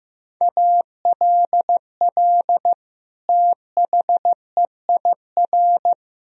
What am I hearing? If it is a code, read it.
Morse code, heard as a keyed tone, about 15 wpm, 700 Hz. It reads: ALL THEIR
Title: Sentences from Top 1000 Words - Review Part 2 - 15wpm